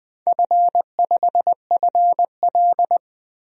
Morse code, heard as a keyed tone, about 20 wpm, 700 Hz. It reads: F5FL
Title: DXCC-100 Call Signs - 4 Characters - Repeated 3x - 20wpm